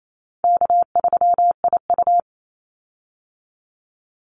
Morse code, heard as a keyed tone, about 28 wpm, 700 Hz. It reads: K3IU
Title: Easy Call Signs Repeated 3x - 28wpm